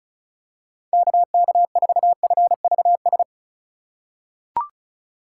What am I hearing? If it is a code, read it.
Morse code, heard as a keyed tone, about 34 wpm, 700 Hz. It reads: KK4FVS
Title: USA Call Signs - 6 Characters - 6x Character Spacing - 35wpm